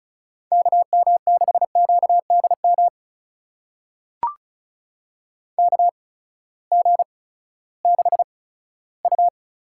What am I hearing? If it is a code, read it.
Morse code, heard as a keyed tone, about 36 wpm, 700 Hz. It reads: K M 6 Q D M K G 6 U
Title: USA Call Signs - 6 Characters - 8x Character Spacing - 35wpm